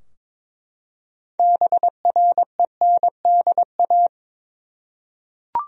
Morse code, heard as a keyed tone, about 22 wpm, 700 Hz. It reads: BRENDA
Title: USA Common Names - 22wpm